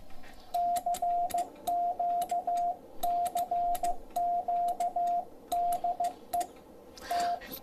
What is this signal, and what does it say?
Morse code, heard as a keyed tone, about 15 wpm, 700 Hz, with radio noise. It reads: CQ CQ DE T